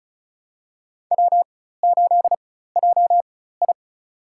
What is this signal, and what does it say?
Morse code, heard as a keyed tone, about 36 wpm, spaced as for 16 wpm, 700 Hz. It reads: W8JI
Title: USA Call Signs - 4 Characters - 4x Character Spacing - 35wpm